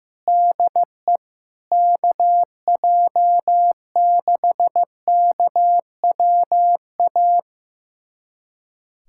Morse code, heard as a keyed tone, about 15 wpm, 700 Hz. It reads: DE KJ6KWA
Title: USA Callsign DE Callsign - 6 Characters - 15wpm